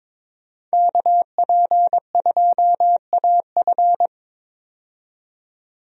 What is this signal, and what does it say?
Morse code, heard as a keyed tone, about 22 wpm, 700 Hz. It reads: KP2AF
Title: USA Call Signs - 5 Characters - 22wpm